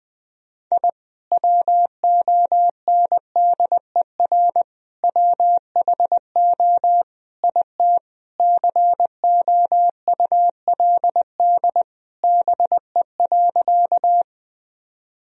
Morse code, heard as a keyed tone, about 20 wpm, 700 Hz. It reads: I WONDER WHO IT COULD BE.
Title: Sentences from Top 1000 Words - Review Part 3 - 20wpm